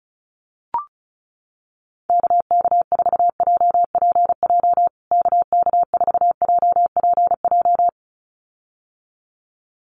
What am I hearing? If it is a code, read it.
Morse code, heard as a keyed tone, about 34 wpm, 700 Hz. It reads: KK4JPJ KK4JPJ
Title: USA Call Signs Repeated 2x 35wpm